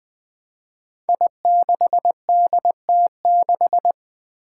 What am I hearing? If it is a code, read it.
Morse code, heard as a keyed tone, about 20 wpm, 700 Hz. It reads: I6DT6
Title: Five Challenging Letter-Number Combinations 20wpm